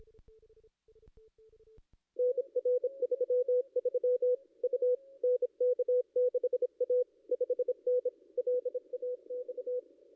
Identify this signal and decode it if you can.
Morse code, heard as a keyed tone, about 26 wpm, 500 Hz. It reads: NR 33 U NK6A 5N LAX
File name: contests / ss-cw-k5tr-op-n5rz -2200